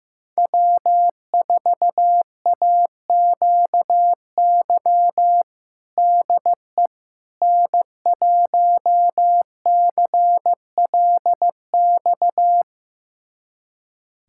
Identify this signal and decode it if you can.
Morse code, heard as a keyed tone, about 16 wpm, 700 Hz. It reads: W4AQY DE N1CLX
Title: USA Callsign DE Callsign - 5 Characters - 15wpm